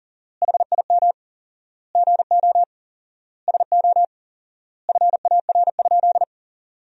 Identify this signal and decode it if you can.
Morse code, heard as a keyed tone, about 40 wpm, 700 Hz. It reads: HIM GO SO FAR?
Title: 4x Word-Spacing - Sentences from Top 300 Words - 40wpm